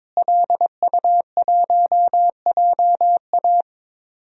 Morse code, heard as a keyed tone, about 22 wpm, 700 Hz. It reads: LU1JA
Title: DXCC-100 Call Signs - 5 Characters - Repeated 3x - 22wpm